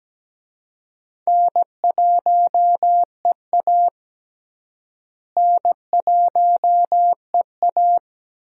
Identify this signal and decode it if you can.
Morse code, heard as a keyed tone, about 17 wpm, 700 Hz. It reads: N1EA N1EA
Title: USA Call Signs - 4 Characters - Repeated 2x - 17wpm